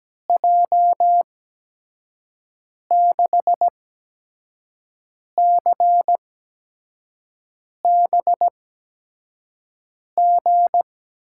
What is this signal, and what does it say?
Morse code, heard as a keyed tone, about 17 wpm, spaced as for 5 wpm, 700 Hz. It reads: J6CBG
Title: USA Call Signs - 6 Characters - 8x Character Spacing - 17wpm